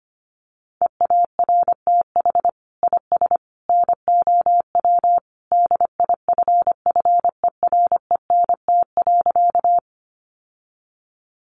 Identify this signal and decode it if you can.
Morse code, heard as a keyed tone, about 25 wpm, 700 Hz. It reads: EARTH IS NOW DIFFERENT.
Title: Sentences from Top 600 Words 25wpm